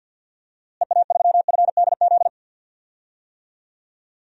Morse code, heard as a keyed tone, about 52 wpm, 700 Hz. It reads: EA3FDZ
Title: DXCC-100 Call Signs - 6 Characters - Repeated 3x - 50wpm